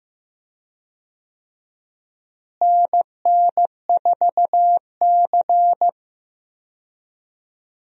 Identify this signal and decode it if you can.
Morse code, heard as a keyed tone, about 15 wpm, 700 Hz. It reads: NN4C
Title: USA Call Signs - 4 Characters - Repeated 3x - 15wpm